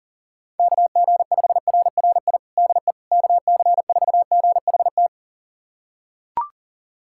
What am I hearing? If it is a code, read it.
Morse code, heard as a keyed tone, about 40 wpm, 700 Hz. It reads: KG5RRI DE KC4GHT
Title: USA Callsign DE Callsign - 6 Characters - 40wpm